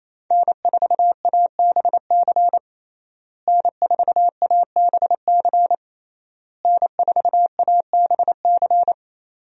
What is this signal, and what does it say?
Morse code, heard as a keyed tone, about 27 wpm, 700 Hz. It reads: N4ABC N4ABC N4ABC
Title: USA Call Signs - 5 Characters - Repeated 3x - 28wpm